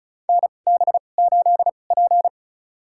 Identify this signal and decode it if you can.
Morse code, heard as a keyed tone, about 36 wpm, spaced as for 25 wpm, 700 Hz. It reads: NB8P
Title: USA Call Signs - 2x Character Spacing - 35wpm